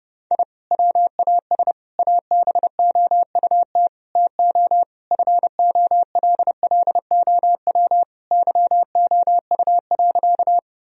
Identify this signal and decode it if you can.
Morse code, heard as a keyed tone, about 32 wpm, 700 Hz. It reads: I WAS ABOUT TO FOLLOW YOU.
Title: Sentences from Top 1000 Words - Review Part 1 - 30wpm